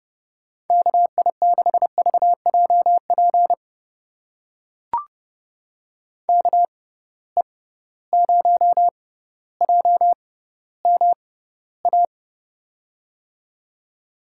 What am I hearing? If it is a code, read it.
Morse code, heard as a keyed tone, about 30 wpm, spaced as for 15 wpm, 700 Hz. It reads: KI6VJP KE0JMA
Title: USA Call Signs - 6 Characters - 6x Character Spacing - 30wpm